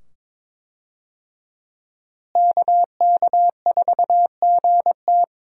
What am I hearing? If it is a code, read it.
Morse code, heard as a keyed tone, about 22 wpm, 700 Hz. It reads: KK4GT
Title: USA Call Signs - 5 Characters - Repeated 2x - 22wpm